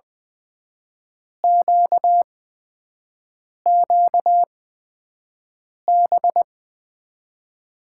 Morse code, heard as a keyed tone, about 20 wpm, spaced as for 5 wpm, 700 Hz. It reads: QQB
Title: USA Call Signs - 8x Character Spacing - 20wpm